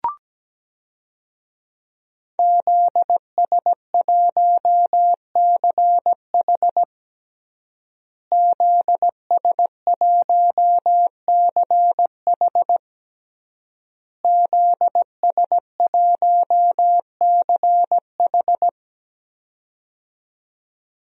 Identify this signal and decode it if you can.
Morse code, heard as a keyed tone, about 17 wpm, 700 Hz. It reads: ZS1CH ZS1CH ZS1CH
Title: DXCC-100 Call Signs - 5 Characters - Repeated 3x - 17wpm